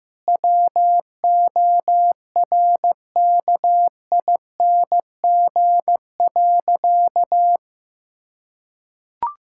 Morse code, heard as a keyed tone, about 15 wpm, 700 Hz. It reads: WORKING.
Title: Sentences from Top 1000 Words - Review Part 6 - 15wpm